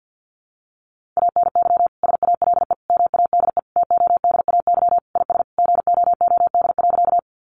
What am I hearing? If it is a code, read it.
Morse code, heard as a keyed tone, about 50 wpm, 700 Hz. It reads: ANY SALE MADE TODAY IS GOOD.
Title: Sentences from Top 1400 Words 50wpm